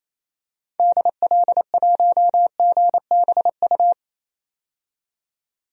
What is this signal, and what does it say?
Morse code, heard as a keyed tone, about 27 wpm, 700 Hz. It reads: DL1GBU
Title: DXCC-100 Call Signs - 6 Characters - 28wpm